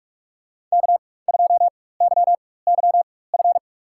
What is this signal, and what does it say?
Morse code, heard as a keyed tone, about 43 wpm, spaced as for 21 wpm, 700 Hz. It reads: K2YYF
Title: USA Call Signs - 5 Characters - 4x Character Spacing - 45wpm